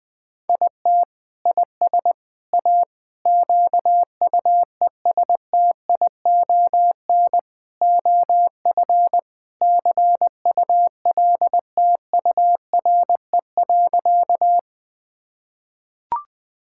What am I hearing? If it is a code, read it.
Morse code, heard as a keyed tone, about 20 wpm, 700 Hz. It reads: IT IS A QUESTION OF CULTURE.
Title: Sentences from Top 1000 Words - Review Part 6 - 20wpm